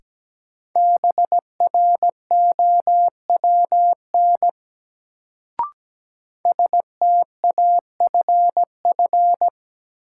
Morse code, heard as a keyed tone, about 17 wpm, 700 Hz. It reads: BROWN STAFF
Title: Sentences from Top 1200 Words - New Word Drill Alt - 17wpm